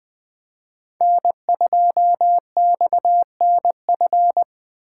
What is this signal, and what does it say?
Morse code, heard as a keyed tone, about 24 wpm, 700 Hz. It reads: N2XNF
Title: Speed Racing - USA 1x3 Call Signs 20wpm